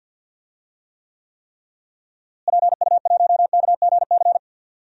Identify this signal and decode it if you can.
Morse code, heard as a keyed tone, about 45 wpm, 700 Hz. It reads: PU1KGC
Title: DXCC-100 Call Signs - 6 Characters - Repeated 3x - 50wpm